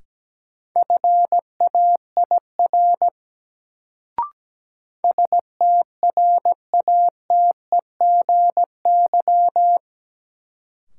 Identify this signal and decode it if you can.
Morse code, heard as a keyed tone, about 17 wpm, 700 Hz. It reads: FAIR STRATEGY
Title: Sentences from Top 1000 Words - Review Part 5 - New Word Drill - 17wpm